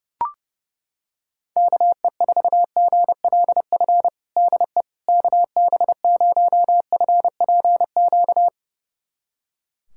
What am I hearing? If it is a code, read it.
Morse code, heard as a keyed tone, about 32 wpm, 700 Hz. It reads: KE4GLF DE KB0FPQ
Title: USA Callsign DE Callsign 30wpm